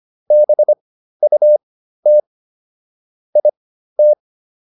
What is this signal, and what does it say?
Morse code, heard as a keyed tone, about 25 wpm, spaced as for 13 wpm, 600 Hz. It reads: BUT IT